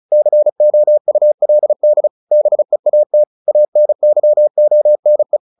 Morse code, heard as a keyed tone, about 35 wpm, 600 Hz. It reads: COULD BEAT ANYONE
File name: the old man and the sea 2 35wpm